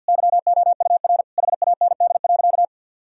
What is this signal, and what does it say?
Morse code, heard as a keyed tone, about 52 wpm, 700 Hz. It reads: YOUR HAND.
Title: Sentences from Top 1000 Words - Review Part 1 - 50wpm